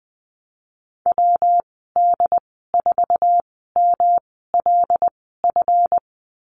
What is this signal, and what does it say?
Morse code, heard as a keyed tone, about 20 wpm, spaced as for 15 wpm, 700 Hz. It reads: WD4MLF
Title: 140 Lesson 23g - Letter f - Call Signs 20wpm